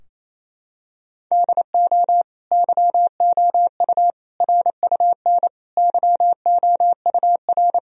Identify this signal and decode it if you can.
Morse code, heard as a keyed tone, about 28 wpm, 700 Hz. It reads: DO YOU RUN YOUR
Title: Sentences from Top 1000 Words - Review Part 4 - 28wpm